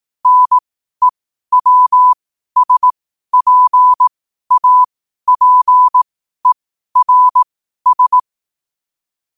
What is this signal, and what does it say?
Morse code, heard as a keyed tone, about 18 wpm, spaced as for 12 wpm, 1000 Hz. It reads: NEWSPAPERS
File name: qotdfiles / 12wpm